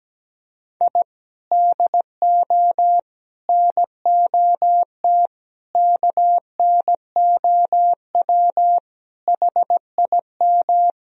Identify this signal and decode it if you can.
Morse code, heard as a keyed tone, about 17 wpm, 700 Hz. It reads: I DO NOT KNOW HIM
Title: Sentences from Top 1000 Words - Review Part 1 - 17wpm